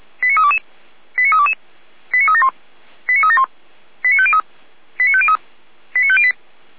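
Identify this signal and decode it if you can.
Five-tone selective call (ZVEI), heard as quick runs of five tones, tones of about 70 ms. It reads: 89322 89322 89371 89371 89503 89503 89508
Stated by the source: ZVEI five-tone selective call received on an SDR.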